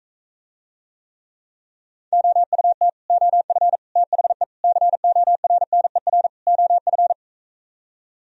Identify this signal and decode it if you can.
Morse code, heard as a keyed tone, about 45 wpm, 700 Hz. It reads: OUT OF THE CORNER OF